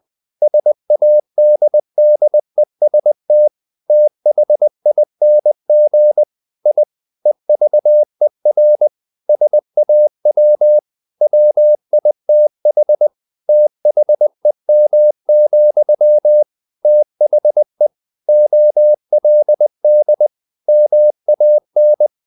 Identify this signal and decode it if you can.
Morse code, heard as a keyed tone, about 20 wpm, 600 Hz. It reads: SADDEST THING I EVER SAW WITH THEM, THE OLD MAN
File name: the old man and the sea / the old man and the sea part 1 20wpm